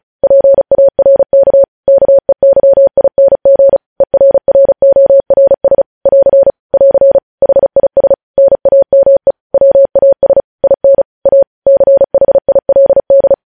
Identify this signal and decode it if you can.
Morse code, heard as a keyed tone, about 36 wpm, 550 Hz. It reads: PARK KEYING ERRORS <AR> <AR> HIS NAME WAS IN A CHILD
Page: News Headlines in Morse Code at 35 WPM